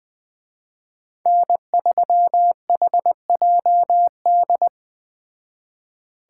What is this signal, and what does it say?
Morse code, heard as a keyed tone, about 20 wpm, 700 Hz. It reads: N3HJD
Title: Speed Racing - USA 1x3 Call Signs 17wpm